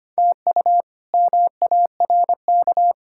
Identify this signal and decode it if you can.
Morse code, heard as a keyed tone, about 25 wpm, 700 Hz. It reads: TU MARK
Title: K1USN SST QSO Practice - 25wpm